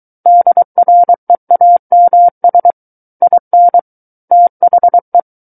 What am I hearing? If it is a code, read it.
Morse code, heard as a keyed tone, about 23 wpm, 700 Hz. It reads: DREAMS IN THE
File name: Book One / 14-Chapter 14 - 23 WPM